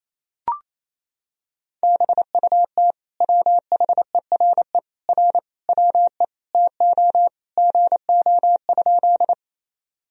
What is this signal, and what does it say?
Morse code, heard as a keyed tone, about 28 wpm, 700 Hz. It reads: BUT WHERE R WE TO GO?
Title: Sentences from Top 200 Words and Abbreviations 28wpm